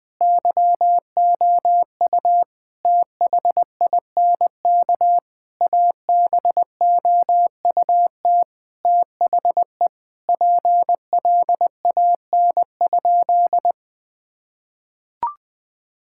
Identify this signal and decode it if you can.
Morse code, heard as a keyed tone, about 21 wpm, 700 Hz. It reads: YOU THINK ABOUT THE PLAN?
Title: Sentences with 5-Word Phrases - 20wpm